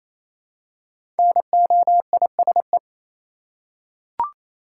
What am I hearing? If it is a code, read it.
Morse code, heard as a keyed tone, about 28 wpm, 700 Hz. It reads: NOISE
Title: Five Letter Words - 28wpm